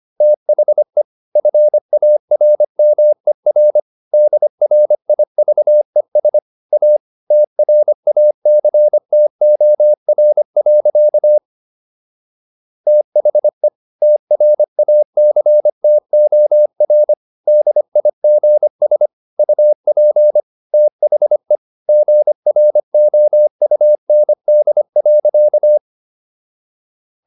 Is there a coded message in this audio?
Yes: Morse code from 0.2 to 25.8 s, heard as a keyed tone, about 25 wpm, 600 Hz. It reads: THE FARMER DRIVES A TRACTOR. THE TRACTOR DIGS UP THE GROUND.